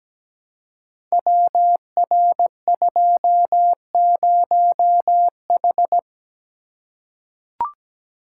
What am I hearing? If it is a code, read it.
Morse code, heard as a keyed tone, about 17 wpm, 700 Hz. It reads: WR20H
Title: Code Groups - 5 Letter:Numbers - 17wpm